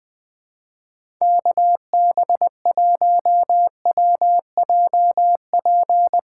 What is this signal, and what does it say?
Morse code, heard as a keyed tone, about 20 wpm, 700 Hz. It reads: KB1WJP
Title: USA Call Signs - 6 Characters - 3x Character Spacing - 20wpm